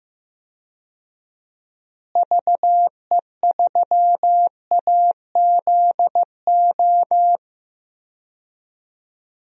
Morse code, heard as a keyed tone, about 15 wpm, 700 Hz. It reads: VE3AZO
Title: DXCC-100 Call Signs - 6 Characters - Repeated 2x - 15wpm